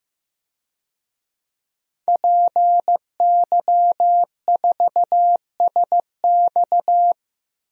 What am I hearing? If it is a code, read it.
Morse code, heard as a keyed tone, about 15 wpm, 700 Hz. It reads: PY4SX
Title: DXCC-100 Call Signs - 5 Characters - 15wpm